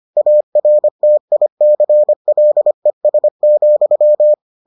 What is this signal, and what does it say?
Morse code, heard as a keyed tone, about 25 wpm, 600 Hz. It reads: ARTICLES,